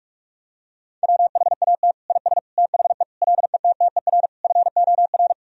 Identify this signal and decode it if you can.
Morse code, heard as a keyed tone, about 44 wpm, 700 Hz. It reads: WHAT IS THE LETTER FOR
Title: Sentences from Top 1000 Words - Review Part 2 - 45wpm